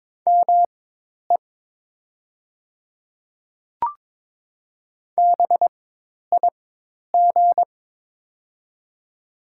Morse code, heard as a keyed tone, about 22 wpm, spaced as for 10 wpm, 700 Hz. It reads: ME BIG
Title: 4x Character Spacing - Top 300 Words - 22wpm